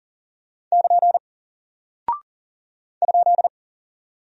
Morse code, heard as a keyed tone, about 40 wpm, 700 Hz. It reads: <KN> ?
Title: Single Character (Letter:Number:Punctuation:Prosign) Alt 40wpm